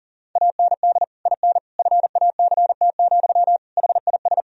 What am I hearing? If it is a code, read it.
Morse code, heard as a keyed tone, about 37 wpm, 700 Hz. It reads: AND IN FACT, HIS
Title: Sentences from Top 1000 Words - Review Part 3 - 40wpm